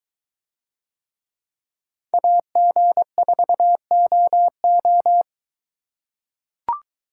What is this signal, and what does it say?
Morse code, heard as a keyed tone, about 23 wpm, 700 Hz. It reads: AG4OO